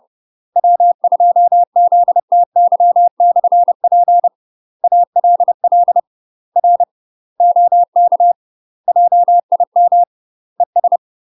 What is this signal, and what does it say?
Morse code, heard as a keyed tone, about 30 wpm, 700 Hz. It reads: W2ZTY/P ALL R OK JIM ES
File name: UK MASTER QSOs TRAINING / 30WPM / UK MASTER QSOs 0045